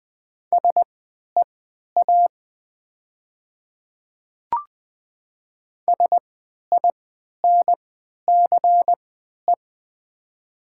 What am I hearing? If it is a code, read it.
Morse code, heard as a keyed tone, about 20 wpm, spaced as for 11 wpm, 700 Hz. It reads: SEA SINCE
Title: 3x Character Spacing - Top 300 Words - 20wpm